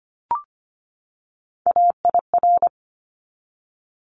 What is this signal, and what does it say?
Morse code, heard as a keyed tone, about 25 wpm, 700 Hz. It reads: AIR
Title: Top 200 Words 25wpm